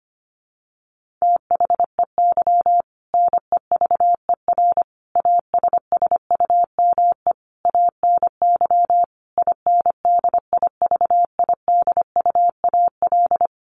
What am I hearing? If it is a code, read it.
Morse code, heard as a keyed tone, about 25 wpm, 700 Hz. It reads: THEY NEVER ASSUME ANY INDIVIDUAL
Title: Sentences from Top 1500 Words 25wpm